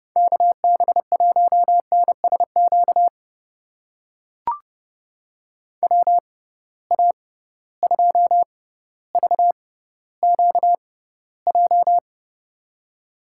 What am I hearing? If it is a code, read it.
Morse code, heard as a keyed tone, about 32 wpm, spaced as for 16 wpm, 700 Hz. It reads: KB1NSQ WA2VQJ
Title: USA Call Signs - 6 Characters - 6x Character Spacing - 30wpm